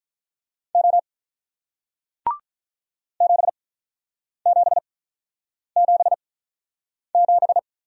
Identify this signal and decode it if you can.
Morse code, heard as a keyed tone, about 44 wpm, spaced as for 8 wpm, 690 Hz. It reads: O 7777